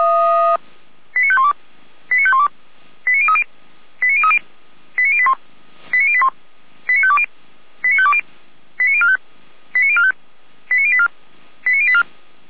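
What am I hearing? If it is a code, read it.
Five-tone selective call (ZVEI), heard as quick runs of five tones, tones of about 70 ms. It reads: 89512 89512 89030 89030 89081 89081 89520 89520 89045 89045 89084 89084
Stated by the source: ZVEI five-tone selective call received by SDR.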